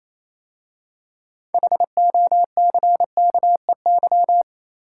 Morse code, heard as a keyed tone, about 28 wpm, 690 Hz. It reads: HOCKEY